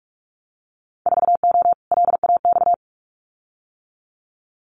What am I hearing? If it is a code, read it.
Morse code, heard as a keyed tone, about 46 wpm, 700 Hz. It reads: 4O LAX
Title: Winter Field Day Exchanges Encore 45wpm